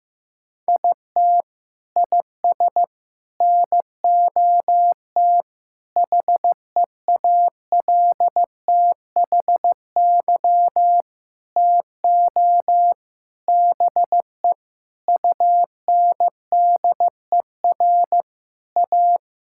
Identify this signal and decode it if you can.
Morse code, heard as a keyed tone, about 15 wpm, 700 Hz. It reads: IT IS NOT HEALTHY TO BE UNDER A
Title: Sentences from Top 1000 Words - Review Part 6 - 15wpm